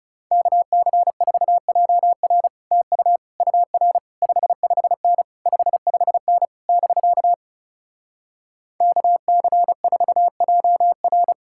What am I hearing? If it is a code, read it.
Morse code, heard as a keyed tone, about 40 wpm, 700 Hz. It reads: KC4JR TU UR 55N 55N <BK> KC4JR